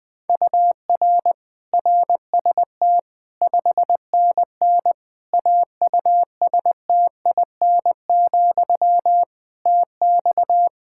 Morse code, heard as a keyed tone, about 20 wpm, 700 Hz. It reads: UR RST 5NN AUSTIN, TX
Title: Practice QSO - Exchange 1 - Beginning - 20wpm